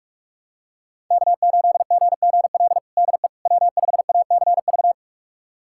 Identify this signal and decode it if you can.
Morse code, heard as a keyed tone, about 42 wpm, 700 Hz. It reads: K8GGL DE W5ACV
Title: USA Callsign DE Callsign - 5 Characters - 45wpm